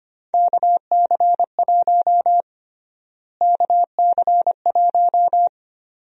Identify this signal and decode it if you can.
Morse code, heard as a keyed tone, about 25 wpm, 700 Hz. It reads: KC1 KC1
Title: USA Call Sign Prefixes - Repeated 2x - 25wpm